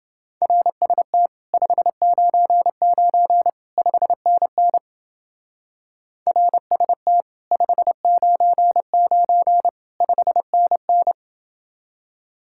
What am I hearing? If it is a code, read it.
Morse code, heard as a keyed tone, about 28 wpm, 700 Hz. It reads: RST 599 5NN RST 599 5NN